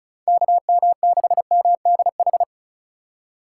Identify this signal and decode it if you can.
Morse code, heard as a keyed tone, about 34 wpm, 690 Hz. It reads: KM6MDH
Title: USA Call Signs - 6 Characters - 6x Character Spacing - 35wpm